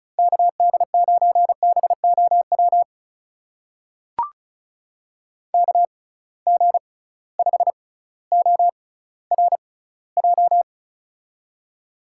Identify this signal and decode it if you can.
Morse code, heard as a keyed tone, about 36 wpm, spaced as for 18 wpm, 700 Hz. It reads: KD9BOW KG5ORJ
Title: USA Call Signs - 6 Characters - 6x Character Spacing - 35wpm